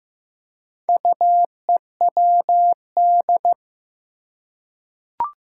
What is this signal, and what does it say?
Morse code, heard as a keyed tone, about 15 wpm, 700 Hz. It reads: UEWD
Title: Four Challenging Letter-Number Combinations 15wpm